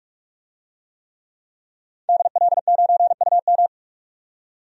Morse code, heard as a keyed tone, about 43 wpm, 700 Hz. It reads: DL9UM
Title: DXCC-100 Call Signs - 5 Characters - 45wpm